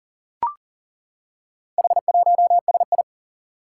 Morse code, heard as a keyed tone, about 40 wpm, 700 Hz. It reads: H1SI
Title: Four Challenging Letter-Number Combinations 40wpm